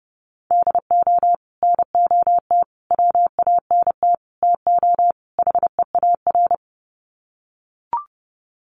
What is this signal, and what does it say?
Morse code, heard as a keyed tone, about 30 wpm, 700 Hz. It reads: DO NOT WANT TO HEAR
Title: Sets of 5 Words 30wpm